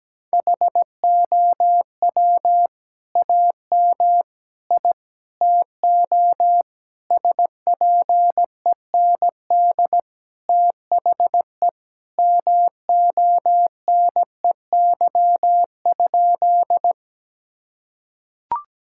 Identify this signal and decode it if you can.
Morse code, heard as a keyed tone, about 17 wpm, 700 Hz. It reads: HOW AM I TO SPEND THE MONEY?
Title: Sentences from Top 1000 Words - Review Part 5 - 17wpm